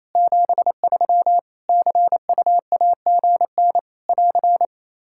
Morse code, heard as a keyed tone, about 27 wpm, 700 Hz. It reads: 73 CUAGN <AR>
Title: Practice QSO - Exchange 4 - Ending - 28wpm